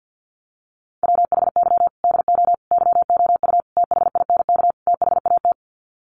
Morse code, heard as a keyed tone, about 48 wpm, 700 Hz. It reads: WHY DO YOU THINK THAT
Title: Sets of 5 Words Encore2 50wpm